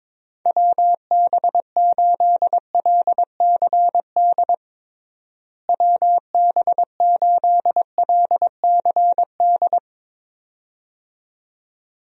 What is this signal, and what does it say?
Morse code, heard as a keyed tone, about 22 wpm, 700 Hz. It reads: WB8LCD WB8LCD
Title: USA Call Signs - 6 Characters - Repeated 2x - 22wpm